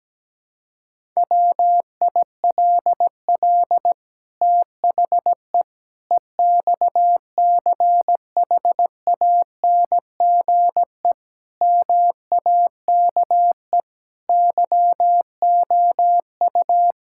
Morse code, heard as a keyed tone, about 17 wpm, 700 Hz. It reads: WILL THE EXCHANGE MAKE YOU
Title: Sentences from Top 1000 Words - Review Part 6 - 17wpm